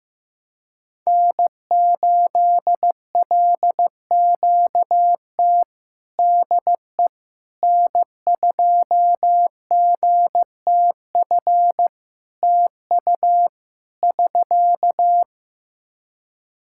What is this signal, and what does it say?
Morse code, heard as a keyed tone, about 15 wpm, 700 Hz. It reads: N8LQT DE N2GTF TU <SK>
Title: K1USN SST QSO Practice 15wpm